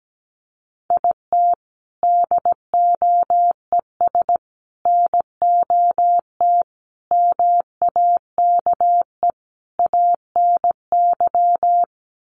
Morse code, heard as a keyed tone, about 17 wpm, 700 Hz. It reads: IT DOES NOT MAKE ANY
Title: Sets of 5 Words Encore2 17wpm